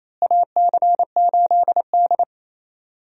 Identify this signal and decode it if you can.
Morse code, heard as a keyed tone, about 27 wpm, 700 Hz. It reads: AC8D
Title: USA Call Signs - 4 Characters - Repeated 3x - 28wpm